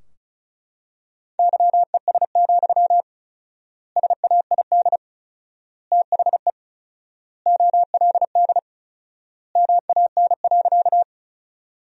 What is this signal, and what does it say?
Morse code, heard as a keyed tone, about 34 wpm, 700 Hz. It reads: YES, SAID THE OLD MAN.
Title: 4x Word-Spacing - Sentences from Top 500 Words - 35wpm